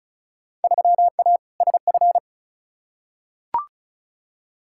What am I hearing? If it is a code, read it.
Morse code, heard as a keyed tone, about 36 wpm, 700 Hz. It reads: 3A SF
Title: Field Day Exchanges Encore - 35wpm